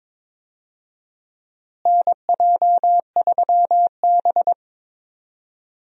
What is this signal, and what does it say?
Morse code, heard as a keyed tone, about 22 wpm, 700 Hz. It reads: NJ3B
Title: USA Call Signs - 4 Characters - Repeated 3x - 22wpm